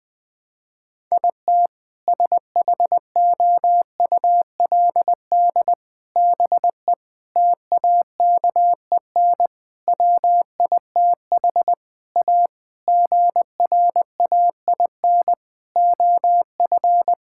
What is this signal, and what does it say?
Morse code, heard as a keyed tone, about 20 wpm, 700 Hz. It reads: IT SHOULD BE TAKEN WITH A GRAIN OF
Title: Sentences with 7-Word Phrases - Spoken Beginning - 20wpm